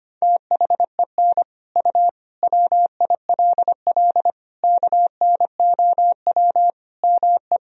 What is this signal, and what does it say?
Morse code, heard as a keyed tone, about 25 wpm, 700 Hz. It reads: THEN U WILL KNOW ME
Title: Sentences from Top 400 Words and Abbreviations 25wpm